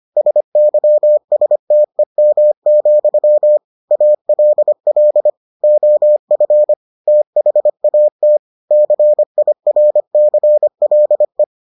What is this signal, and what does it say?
Morse code, heard as a keyed tone, about 25 wpm, 600 Hz. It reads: SYSTEM, ALL OF THAT CIRCLE